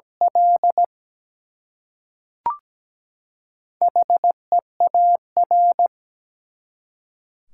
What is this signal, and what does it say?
Morse code, heard as a keyed tone, about 17 wpm, 700 Hz. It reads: L HEAR
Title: Four Letter Words - 17wpm